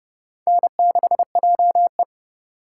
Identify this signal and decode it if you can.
Morse code, heard as a keyed tone, about 30 wpm, 700 Hz. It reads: N6JE
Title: USA Call Signs - 4 Characters - 4x Character Spacing - 30wpm